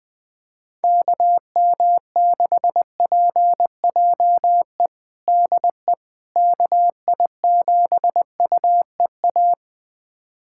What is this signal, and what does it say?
Morse code, heard as a keyed tone, about 20 wpm, 700 Hz. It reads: KM6PJE DE KI7UEA
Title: USA Callsign DE Callsign 20wpm